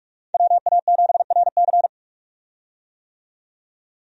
Morse code, heard as a keyed tone, about 46 wpm, 700 Hz. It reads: WA7RC
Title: USA Call Signs - 5 Characters - 45wpm